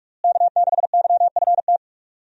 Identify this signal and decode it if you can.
Morse code, heard as a keyed tone, about 43 wpm, 700 Hz. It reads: K6YFT
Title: USA Call Signs - 5 Characters - Repeated 3x - 45wpm